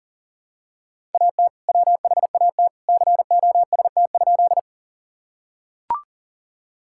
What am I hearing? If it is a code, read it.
Morse code, heard as a keyed tone, about 40 wpm, 700 Hz. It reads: AT WHAT COST?
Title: Sentences from Top 1000 Words - Review Part 5 - 40wpm